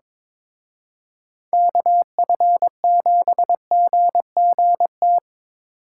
Morse code, heard as a keyed tone, about 26 wpm, 700 Hz. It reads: KF7GGT
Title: Speed Racing - USA 2x3 Call Signs 22wpm